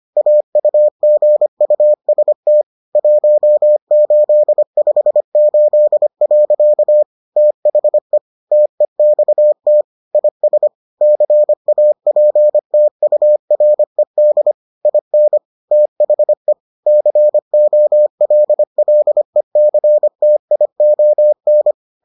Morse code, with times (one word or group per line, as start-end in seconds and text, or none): 0.2-2.6 AUGUST
3.0-7.0 1858.
7.4-8.2 THE
8.5-9.8 TEXT
10.1-10.7 IS
11.0-14.5 CAPTURED
14.9-15.4 IN
15.7-16.5 THE
16.9-21.7 COLLECTION